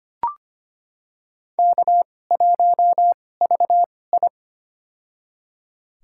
Morse code, heard as a keyed tone, about 25 wpm, spaced as for 18 wpm, 700 Hz. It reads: K1VI
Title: USA Call Signs - 4 Characters - 2x Character Spacing - 25wpm